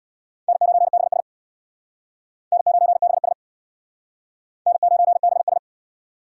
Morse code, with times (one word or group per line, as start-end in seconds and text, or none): 0.5-1.2 N9BS
2.5-3.3 N9BS
4.7-5.6 N9BS